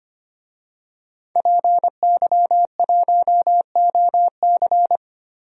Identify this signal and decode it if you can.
Morse code, heard as a keyed tone, about 25 wpm, 700 Hz. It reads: PY1OC
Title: DXCC-100 Call Signs - 5 Characters - 25wpm